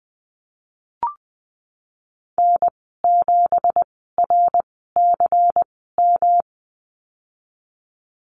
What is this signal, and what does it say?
Morse code, heard as a keyed tone, about 20 wpm, spaced as for 15 wpm, 700 Hz. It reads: N7RCM
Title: 167 Lesson 27g - Number 7 - Call Signs 20wpm